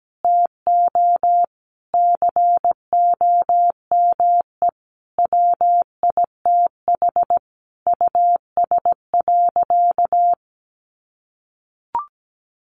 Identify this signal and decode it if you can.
Morse code, heard as a keyed tone, about 17 wpm, 700 Hz. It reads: TO COME WITH US.
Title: Sentences from Top 1300 Words 17wpm